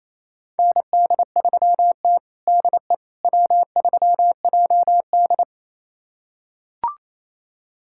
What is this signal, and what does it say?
Morse code, heard as a keyed tone, about 28 wpm, 700 Hz. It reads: ND3T DE W3JD
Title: USA Callsign DE Callsign - 4 Characters - 28wpm